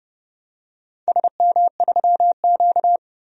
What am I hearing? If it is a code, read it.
Morse code, heard as a keyed tone, about 30 wpm, 700 Hz. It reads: SM3Q
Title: DXCC-100 Call Signs - 4 Characters - 30wpm